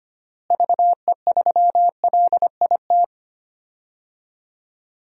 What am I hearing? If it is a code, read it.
Morse code, heard as a keyed tone, about 25 wpm, 700 Hz. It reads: VE3LIT
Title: DXCC-100 Call Signs - 6 Characters - Repeated 3x - 25wpm